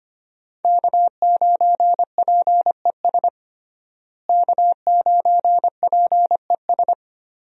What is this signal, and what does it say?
Morse code, heard as a keyed tone, about 25 wpm, 700 Hz. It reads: K9PES K9PES
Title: USA Call Signs - 5 Characters - Repeated 2x - 25wpm